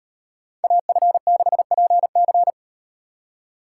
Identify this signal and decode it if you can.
Morse code, heard as a keyed tone, about 38 wpm, 700 Hz. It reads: AF6PC